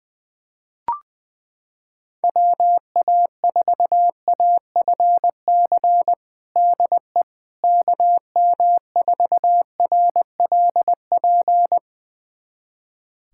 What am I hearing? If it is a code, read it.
Morse code, heard as a keyed tone, about 20 wpm, 700 Hz. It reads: WA4AFC DE KM4RLP
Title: USA Callsign DE Callsign - 6 Characters - 20wpm